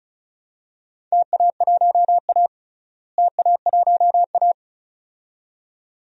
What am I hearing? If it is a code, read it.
Morse code, heard as a keyed tone, about 36 wpm, 700 Hz. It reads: TA1A TA1A
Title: DXCC-100 Call Signs - 4 Characters - Repeated 2x - 35wpm